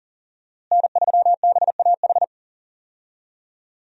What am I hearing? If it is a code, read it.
Morse code, heard as a keyed tone, about 40 wpm, 700 Hz. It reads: N3BAH
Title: USA Call Signs - 5 Characters - Repeated 2x - 40wpm